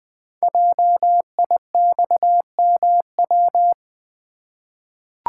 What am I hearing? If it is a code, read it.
Morse code, heard as a keyed tone, about 20 wpm, 700 Hz. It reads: JIXMW
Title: Code Groups - 5 Letters - 20wpm